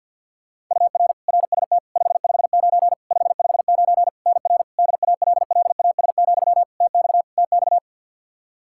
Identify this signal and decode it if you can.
Morse code, heard as a keyed tone, about 52 wpm, 700 Hz. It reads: UR RST 559 559 NR DALLAS, TX TX
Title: Practice QSO - Exchange 1 - Beginning - 50wpm